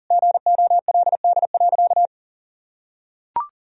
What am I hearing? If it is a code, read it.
Morse code, heard as a keyed tone, about 40 wpm, 700 Hz. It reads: GOLD.
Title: Sentences from Top 1000 Words - Review Part 3 - 40wpm